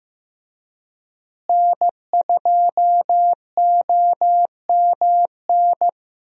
Morse code, heard as a keyed tone, about 15 wpm, 700 Hz. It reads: N2OMN
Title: USA Call Signs - 5 Characters - Repeated 3x - 15wpm